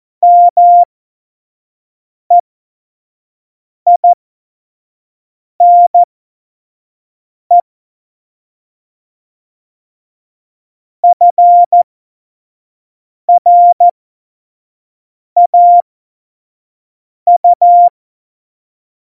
Morse code, heard as a keyed tone, about 14 wpm, spaced as for 5 wpm, 700 Hz. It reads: MEINE FRAU
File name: Morsekurs / Witze / Witz 06 March 2018, 25 F70